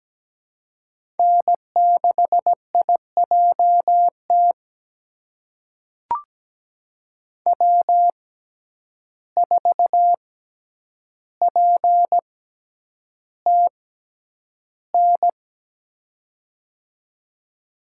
Morse code, heard as a keyed tone, about 17 wpm, spaced as for 9 wpm, 700 Hz. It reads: N6IJT W4PTN
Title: USA Call Signs - 5 Characters - 6x Character Spacing - 17wpm